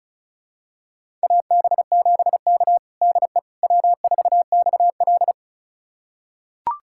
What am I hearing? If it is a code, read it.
Morse code, heard as a keyed tone, about 36 wpm, 700 Hz. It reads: AB7K DE W4XL
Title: USA Callsign DE Callsign - 4 Characters - 35wpm